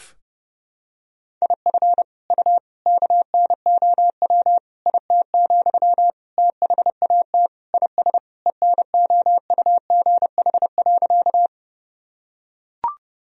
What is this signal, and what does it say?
Morse code, heard as a keyed tone, about 30 wpm, 700 Hz. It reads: IF U KNOW IT, THAT IS ENOUGH.
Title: Sentences from Top 400 Words and Abbreviations 30wpm